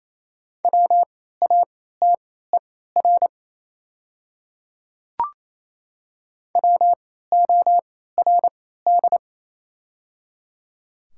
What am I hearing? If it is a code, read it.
Morse code, heard as a keyed tone, about 28 wpm, spaced as for 16 wpm, 700 Hz. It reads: WATER WORD
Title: 3x Character Spacing - Top 100 Words - 28wpm